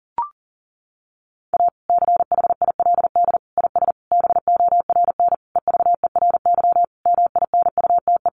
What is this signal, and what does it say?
Morse code, heard as a keyed tone, about 37 wpm, 700 Hz. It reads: A CHILD IS BORN EVERY MINUTE
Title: Sentences from Top 1400 Words 40wpm